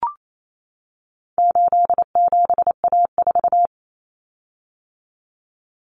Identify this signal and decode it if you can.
Morse code, heard as a keyed tone, about 28 wpm, 700 Hz. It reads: 87A4
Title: Common Cut Numbers - 4-Digit 28wpm